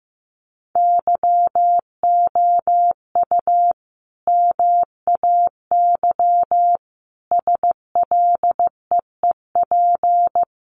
Morse code, heard as a keyed tone, about 15 wpm, 700 Hz. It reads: YOU MAY SLEEP
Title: Sentences from Top 1500 Words 15wpm